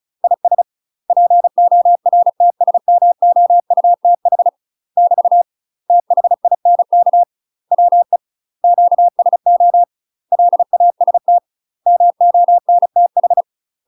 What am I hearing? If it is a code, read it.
Morse code, heard as a keyed tone, about 36 wpm, 700 Hz. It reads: IS PORTSMOUTH <BT> THINK WE QSO LAST MONTH
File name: UK MASTER QSOs TRAINING / 35WPM / UK MASTER QSOs 0042